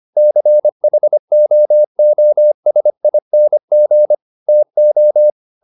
Morse code, heard as a keyed tone, about 25 wpm, 600 Hz. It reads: CHOOSING TO